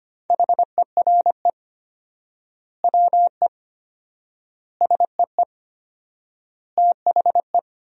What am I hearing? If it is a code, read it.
Morse code, heard as a keyed tone, about 25 wpm, 700 Hz. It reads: HERE WE SEE THE
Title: 4x Word-Spacing - Sentences from Top 300 Words - 25wpm